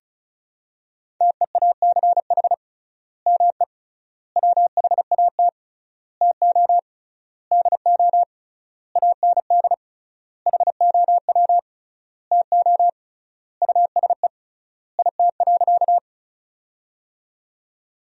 Morse code, heard as a keyed tone, about 36 wpm, 700 Hz. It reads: TEACH ME WHAT TO DO AND HOW TO USE IT.
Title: 3x Word-Spacing - Sentences from Top 500 Words - 35wpm